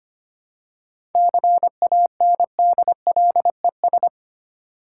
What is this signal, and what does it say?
Morse code, heard as a keyed tone, about 25 wpm, 690 Hz. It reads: CANDLES
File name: practice / 25wpm / Easy Words with Hints - 25wpm